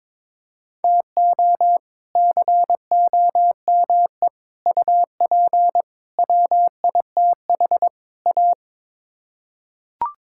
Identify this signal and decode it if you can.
Morse code, heard as a keyed tone, about 22 wpm, 700 Hz. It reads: TO COME UP WITH A
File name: practice / 22wpm / Build A Phrase - 5-Word Phrases - 22wpm